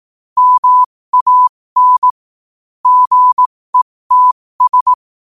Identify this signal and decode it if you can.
Morse code, heard as a keyed tone, about 18 wpm, spaced as for 15 wpm, 1000 Hz. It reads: MAN GETS
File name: qotdfiles / 15wpm